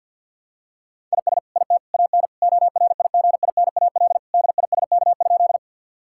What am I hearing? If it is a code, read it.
Morse code, heard as a keyed tone, about 50 wpm, 700 Hz. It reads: IS IT AN ORIGINAL DISK?
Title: Sentences from Top 1000 Words - Review Part 6 - 50wpm